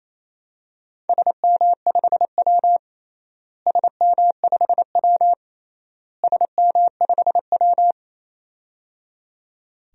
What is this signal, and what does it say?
Morse code, heard as a keyed tone, about 27 wpm, 700 Hz. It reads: SM5W SM5W SM5W
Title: DXCC-100 Call Signs - 4 Characters - Repeated 3x - 28wpm